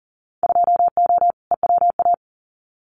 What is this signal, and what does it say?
Morse code, heard as a keyed tone, about 40 wpm, 700 Hz. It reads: 2O EWA
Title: Winter Field Day Exchanges Encore 40wpm